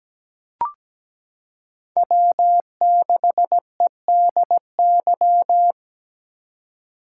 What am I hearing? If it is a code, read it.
Morse code, heard as a keyed tone, about 17 wpm, 700 Hz. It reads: W6EDY
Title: K1USN SST QSO Practice - 17wpm